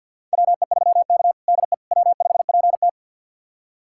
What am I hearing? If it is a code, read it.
Morse code, heard as a keyed tone, about 52 wpm, 700 Hz. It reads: WE3K DE W5PT
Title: USA Callsign DE Callsign - 4 Characters - 50wpm